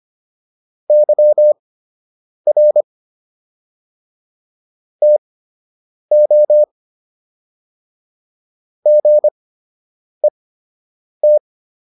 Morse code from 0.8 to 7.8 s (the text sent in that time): YR TO